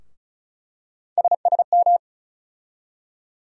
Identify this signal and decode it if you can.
Morse code, heard as a keyed tone, about 36 wpm, 700 Hz. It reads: SSM
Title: USA Call Sign Postfixes - 3 Letters 35wpm